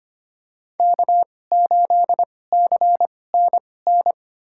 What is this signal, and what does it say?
Morse code, heard as a keyed tone, about 25 wpm, spaced as for 18 wpm, 700 Hz. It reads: K8CNN
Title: USA Call Signs - 5 Characters - 2x Character Spacing - 25wpm